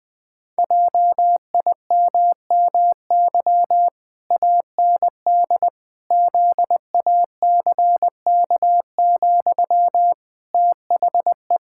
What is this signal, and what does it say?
Morse code, heard as a keyed tone, about 20 wpm, 700 Hz. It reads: JIMMY AND ZACK, THE
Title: Pangram Challenge v2 - 20wpm